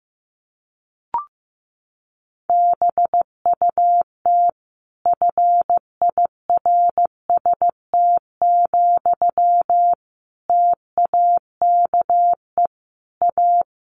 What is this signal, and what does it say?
Morse code, heard as a keyed tone, about 14 wpm, 700 Hz. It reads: BUT FIRST, TAKE A
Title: Sentences from Top 1400 Words 15wpm